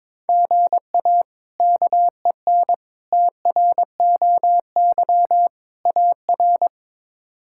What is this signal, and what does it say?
Morse code, heard as a keyed tone, about 22 wpm, 700 Hz. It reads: GA KEN TROY AR
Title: K1USN SST QSO Practice - 22wpm